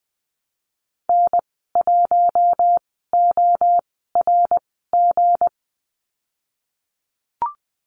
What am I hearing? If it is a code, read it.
Morse code, heard as a keyed tone, about 20 wpm, spaced as for 15 wpm, 700 Hz. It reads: N1ORG
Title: 193 Lesson 31g - Letter v - Call Signs 20wpm